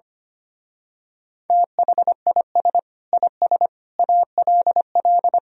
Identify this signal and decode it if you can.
Morse code, heard as a keyed tone, about 24 wpm, 700 Hz. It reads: THIS IS ALL
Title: Speed-Racing - Sets of 5 Words 25wpm